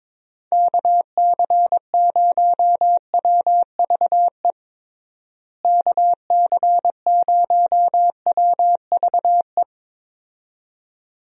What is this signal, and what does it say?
Morse code, heard as a keyed tone, about 22 wpm, 700 Hz. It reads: KC0WVE KC0WVE
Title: USA Call Signs - 6 Characters - Repeated 2x - 22wpm